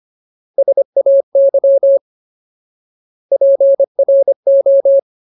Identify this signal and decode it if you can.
Morse code, heard as a keyed tone, about 25 wpm, 550 Hz. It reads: SAY PRO
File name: Ultimate Word List / 25WPM Ultimate